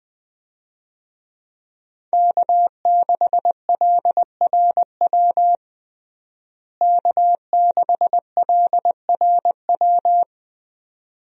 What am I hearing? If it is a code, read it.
Morse code, heard as a keyed tone, about 20 wpm, 700 Hz. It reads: K6LRW K6LRW
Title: USA Call Signs - 5 Characters - Repeated 2x - 20wpm